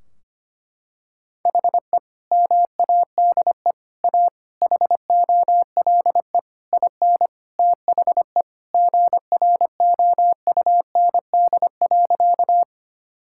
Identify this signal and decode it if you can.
Morse code, heard as a keyed tone, about 25 wpm, 700 Hz. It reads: HE MADE A HOLE IN THE GROUND.
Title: Sentences from Top 1600 Words 25wpm